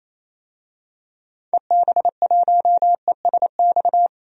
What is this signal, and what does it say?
Morse code, heard as a keyed tone, about 28 wpm, 700 Hz. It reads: EB1ESX
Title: DXCC-100 Call Signs - 6 Characters - 28wpm